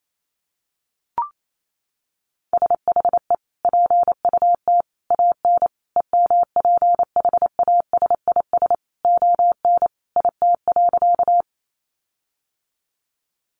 Morse code, heard as a keyed tone, about 28 wpm, 700 Hz. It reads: SHE PUT AN EMPHASIS ON IT.
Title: Sentences from Top 1400 Words 28wpm